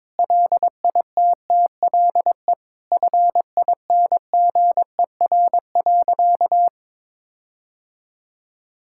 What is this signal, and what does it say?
Morse code, heard as a keyed tone, about 22 wpm, 700 Hz. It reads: LITTLE FINGER.
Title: Sentences from Top 1600 Words 22wpm